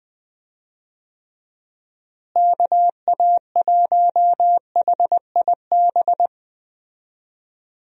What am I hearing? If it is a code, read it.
Morse code, heard as a keyed tone, about 20 wpm, 700 Hz. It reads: KA1HIB
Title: USA Call Signs - 6 Characters - Repeated 2x - 20wpm